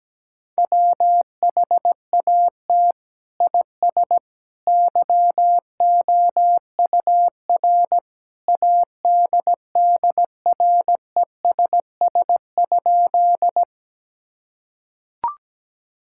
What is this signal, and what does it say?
Morse code, heard as a keyed tone, about 17 wpm, 700 Hz. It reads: WHAT IS YOUR ADDRESS?
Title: Sentences from Top 1000 Words - Review Part 6 - 17wpm